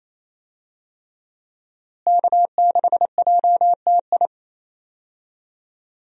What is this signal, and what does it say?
Morse code, heard as a keyed tone, about 28 wpm, 700 Hz. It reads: K6JTI
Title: USA Call Signs - 5 Characters - Repeated 3x - 28wpm